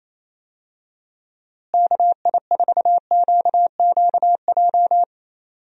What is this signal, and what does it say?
Morse code, heard as a keyed tone, about 28 wpm, 700 Hz. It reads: KI4QQJ
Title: USA Call Signs - 6 Characters - Repeated 3x - 28wpm